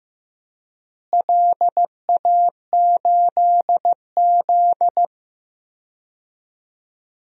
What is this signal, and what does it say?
Morse code, heard as a keyed tone, about 15 wpm, 700 Hz. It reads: LA8Z
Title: DXCC-100 Call Signs - 4 Characters - Repeated 2x - 15wpm